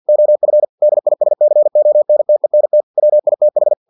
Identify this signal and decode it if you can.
Morse code, heard as a keyed tone, about 50 wpm, 600 Hz. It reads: OF DISCONTENT WITH
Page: Never 49wpm